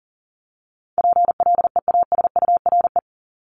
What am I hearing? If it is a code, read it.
Morse code, heard as a keyed tone, about 40 wpm, 700 Hz. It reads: PLEASURE
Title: Top 1501-1750 Words 40wpm